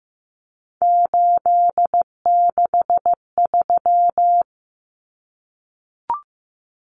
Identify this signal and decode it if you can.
Morse code, heard as a keyed tone, about 15 wpm, 700 Hz. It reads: 863
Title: Common Cut Numbers - 3-Digit 15wpm